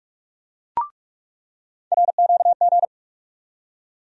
Morse code, heard as a keyed tone, about 43 wpm, 700 Hz. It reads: RQG
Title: USA Call Sign Postfixes - 3 Letters 45wpm